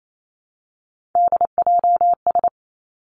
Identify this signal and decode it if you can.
Morse code, heard as a keyed tone, about 28 wpm, 700 Hz. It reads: DJS
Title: Three Challenging Letter-Number Combinations 28wpm